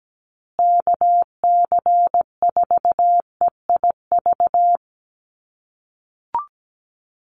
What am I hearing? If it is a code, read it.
Morse code, heard as a keyed tone, about 17 wpm, 700 Hz. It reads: KC4EIV
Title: USA Call Signs Repeated 3x 17wpm